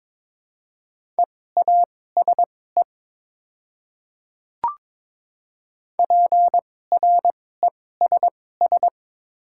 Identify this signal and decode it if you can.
Morse code, heard as a keyed tone, about 22 wpm, spaced as for 16 wpm, 700 Hz. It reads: EASE PRESS
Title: 2x Character Spacing - Top 500 Words - 22wpm